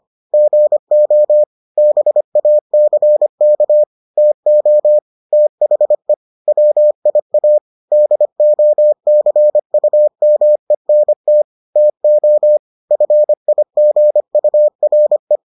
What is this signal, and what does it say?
Morse code, heard as a keyed tone, about 25 wpm, 600 Hz. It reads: GO BACK TO THE WIA DOCUMENT TO FIGURE